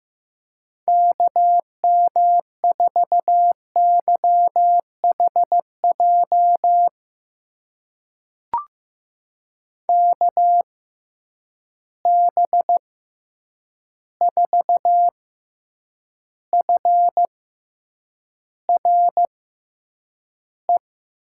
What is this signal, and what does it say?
Morse code, heard as a keyed tone, about 15 wpm, spaced as for 8 wpm, 700 Hz. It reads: KM4YHJ KB4FRE
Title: USA Call Signs - 6 Characters - 6x Character Spacing - 15wpm